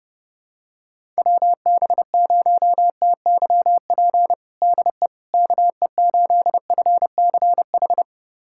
Morse code, heard as a keyed tone, about 30 wpm, 700 Hz. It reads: WB0TYP DE KE8FCH
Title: USA Callsign DE Callsign - 6 Characters - 30wpm